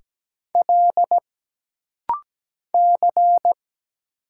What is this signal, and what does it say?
Morse code, heard as a keyed tone, about 17 wpm, 700 Hz. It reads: L C
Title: Single Letters - Rapid-Fire 17wpm